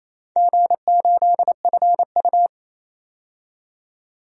Morse code, heard as a keyed tone, about 28 wpm, 700 Hz. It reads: G8FU
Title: DXCC-100 Call Signs - 4 Characters - 28wpm